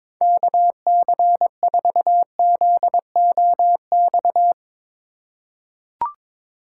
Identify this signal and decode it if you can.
Morse code, heard as a keyed tone, about 22 wpm, 700 Hz. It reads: KC4ZOX
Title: USA Call Signs - 6 Characters - 2x Character Spacing - 22wpm